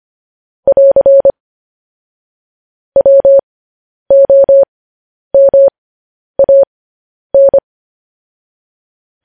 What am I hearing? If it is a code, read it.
Morse code, heard as a keyed tone, about 25 wpm, spaced as for 10 wpm, 550 Hz. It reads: <AR> WOMAN
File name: podcast / News headlines-1x-25x10